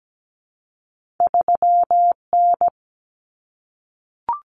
Morse code, heard as a keyed tone, about 17 wpm, 700 Hz. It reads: 3N
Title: Common Cut Numbers - 2-Digit 17wpm